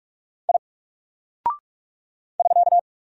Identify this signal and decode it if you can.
Morse code, heard as a keyed tone, about 44 wpm, 700 Hz. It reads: I <SK>
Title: Single Character (Letter:Number:Punctuation:Prosign) 45wpm